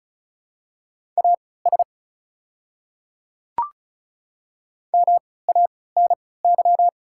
Morse code, heard as a keyed tone, about 36 wpm, spaced as for 20 wpm, 700 Hz. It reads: AS MANY
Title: 3x Character Spacing - Top 100 Words - 35wpm